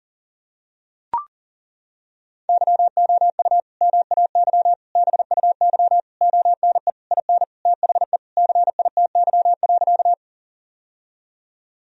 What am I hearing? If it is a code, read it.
Morse code, heard as a keyed tone, about 40 wpm, 700 Hz. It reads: YOU MAY BUY ONE IN THE CITY.
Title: Sentences from Top 1000 Words - Review Part 4 - 40wpm